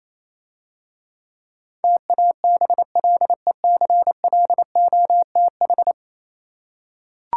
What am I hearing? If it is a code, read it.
Morse code, heard as a keyed tone, about 27 wpm, 700 Hz. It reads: TABLECLOTH
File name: Easy Words - Speed-Racing with Hints - 22wpm